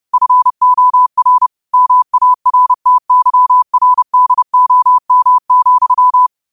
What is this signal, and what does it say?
Morse code, heard as a keyed tone, about 30 wpm, 1000 Hz. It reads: FOR MARTYRDOM,
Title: 30 WPM morse code quotes for Fri, 01 Mar 2024